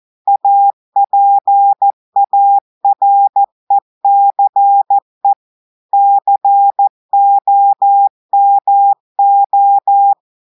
Morse code, heard as a keyed tone, about 14 wpm, 800 Hz. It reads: APARECE COMO
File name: audios / portugais / portugais texte en clair 1 vitesse 14 mots minutes